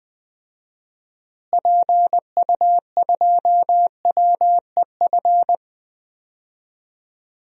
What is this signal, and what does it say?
Morse code, heard as a keyed tone, about 20 wpm, 700 Hz. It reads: PU2WEF
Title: DXCC-100 Call Signs - 6 Characters - Repeated 3x - 20wpm